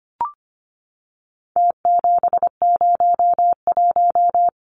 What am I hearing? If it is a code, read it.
Morse code, heard as a keyed tone, about 25 wpm, 700 Hz. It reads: T701
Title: Numbers - Contest Style - 4-Digit 25wpm